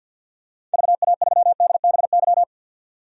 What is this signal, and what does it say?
Morse code, heard as a keyed tone, about 45 wpm, 700 Hz. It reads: VA3DBY
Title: DXCC-100 Call Signs - 6 Characters - 50wpm